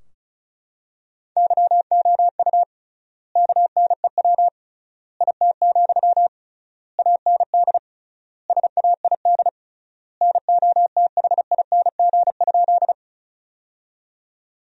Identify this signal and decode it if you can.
Morse code, heard as a keyed tone, about 34 wpm, 700 Hz. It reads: YOU KNEW IT, AND SAID NOTHING?
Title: 3x Word-Spacing - Sentences from Top 500 Words - 35wpm